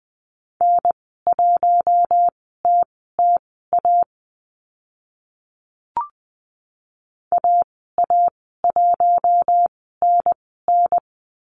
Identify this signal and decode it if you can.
Morse code, heard as a keyed tone, about 20 wpm, spaced as for 14 wpm, 700 Hz. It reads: N1TTA AA1NN
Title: 042 Lesson 8g - Number 1 - Call Signs 20wpm